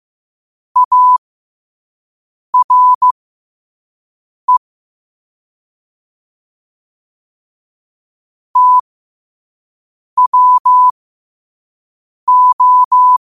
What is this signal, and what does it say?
Morse code, heard as a keyed tone, about 15 wpm, spaced as for 5 wpm, 1000 Hz. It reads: ARE TWO
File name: qotdfiles / 5wpm